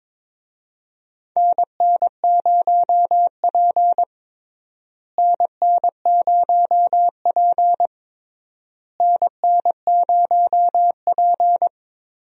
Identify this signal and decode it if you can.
Morse code, heard as a keyed tone, about 22 wpm, 700 Hz. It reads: NN0P NN0P NN0P
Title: USA Call Signs - 4 Characters - Repeated 3x - 22wpm